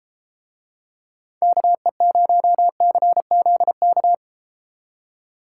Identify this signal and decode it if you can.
Morse code, heard as a keyed tone, about 31 wpm, 700 Hz. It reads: KE0CZK